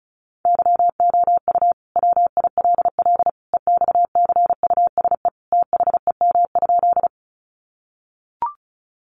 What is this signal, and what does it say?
Morse code, heard as a keyed tone, about 34 wpm, 700 Hz. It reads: YOU WILL EXCUSE THEM?
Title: Sentences from Top 700 Words 35wpm